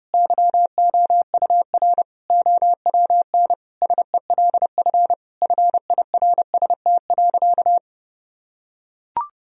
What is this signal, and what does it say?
Morse code, heard as a keyed tone, about 30 wpm, 700 Hz. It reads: YOUR OWN SELF FIRST.
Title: Sentences from Top 1000 Words - Review Part 1 - 30wpm